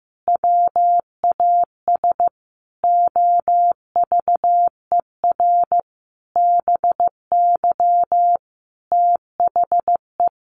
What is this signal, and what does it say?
Morse code, heard as a keyed tone, about 15 wpm, 700 Hz. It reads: WAS OVER BY THE
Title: Sentences from Top 1500 Words 15wpm